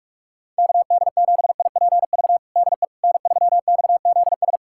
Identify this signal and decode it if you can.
Morse code, heard as a keyed tone, about 46 wpm, 700 Hz. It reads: KD7IPV DE N3XZS
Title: USA Callsign DE Callsign 45wpm